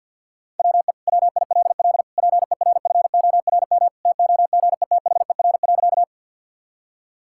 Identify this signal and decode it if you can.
Morse code, heard as a keyed tone, about 50 wpm, 700 Hz. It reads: WE WILL PERFORM TOGETHER.
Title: Sentences from Top 1000 Words - Review Part 5 - 50wpm